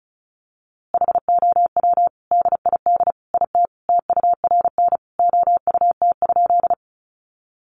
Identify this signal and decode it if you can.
Morse code, heard as a keyed tone, about 34 wpm, 700 Hz. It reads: HOW DID IT TURN OUT?
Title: A Sound Of Thunder 35wpm